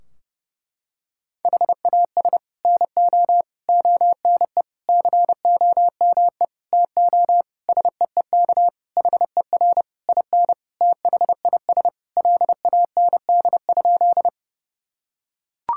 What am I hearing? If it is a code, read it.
Morse code, heard as a keyed tone, about 30 wpm, 700 Hz. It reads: HAS NO ONE COME TO SEEK HER IN THIS LAND?
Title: Sentences from Top 1600 Words 30wpm